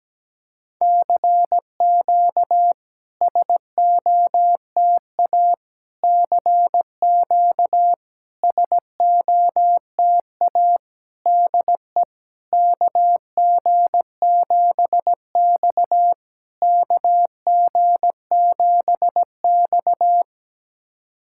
Morse code, heard as a keyed tone, about 17 wpm, 690 Hz. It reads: CQ SOTA CQ SOTA DE KG7X KG7X
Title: SOTA QSO Practice 17wpm